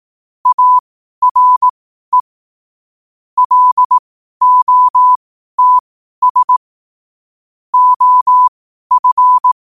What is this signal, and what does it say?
Morse code, heard as a keyed tone, about 18 wpm, spaced as for 12 wpm, 1000 Hz. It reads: ARE LOTS OF
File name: qotdfiles / 12wpm